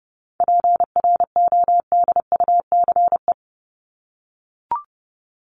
Morse code, heard as a keyed tone, about 30 wpm, 700 Hz. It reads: PRODUCE
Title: ING Suffix - Top 100 Words with Base Words 30wpm